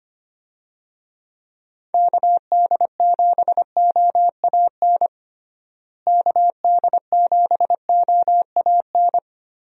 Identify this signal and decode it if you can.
Morse code, heard as a keyed tone, about 25 wpm, 700 Hz. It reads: KD7OAN KD7OAN
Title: USA Call Signs - 6 Characters - Repeated 3x - 25wpm